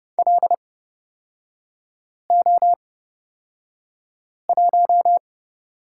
Morse code, heard as a keyed tone, about 31 wpm, spaced as for 5 wpm, 700 Hz. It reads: LO1
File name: Lesson 38e - Letter q - Three Characters 30wpm